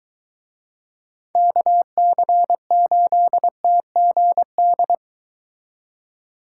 Morse code, heard as a keyed tone, about 22 wpm, 700 Hz. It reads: KC8TGD